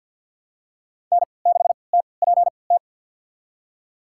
Morse code, heard as a keyed tone, about 53 wpm, spaced as for 29 wpm, 700 Hz. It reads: N6TPT
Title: USA Call Signs - 5 Characters - 3x Character Spacing - 50wpm